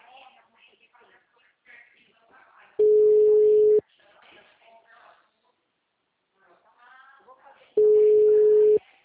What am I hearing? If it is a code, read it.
Ringback tone (Europe, 425 Hz), with speech.